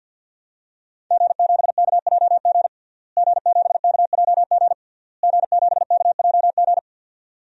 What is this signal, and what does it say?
Morse code, heard as a keyed tone, about 51 wpm, 700 Hz. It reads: G7KJG G7KJG G7KJG
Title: DXCC-100 Call Signs - 5 Characters - Repeated 3x - 50wpm